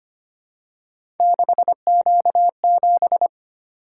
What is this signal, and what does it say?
Morse code, heard as a keyed tone, about 25 wpm, 700 Hz. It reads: 6Q7
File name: Three Letter-Number Combinations - 25wpm